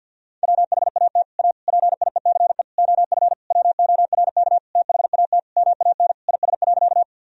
Morse code, heard as a keyed tone, about 52 wpm, 700 Hz. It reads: WHAT A PIECE OF WORK THAT MAN IS.
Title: Sentences from Top 1000 Words - Review Part 2 - 50wpm